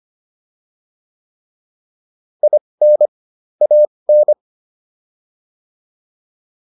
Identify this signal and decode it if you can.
Morse code, heard as a keyed tone, about 25 wpm, spaced as for 20 wpm, 600 Hz. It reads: IN AN